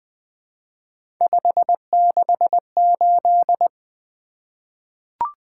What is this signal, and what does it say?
Morse code, heard as a keyed tone, about 20 wpm, 700 Hz. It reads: 568
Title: Three Letter-Number Combinations - 20wpm